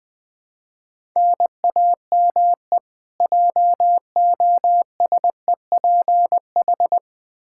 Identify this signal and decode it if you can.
Morse code, heard as a keyed tone, about 20 wpm, 700 Hz. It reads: NAME JOSEPH
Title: K1USN SST QSO Practice 20wpm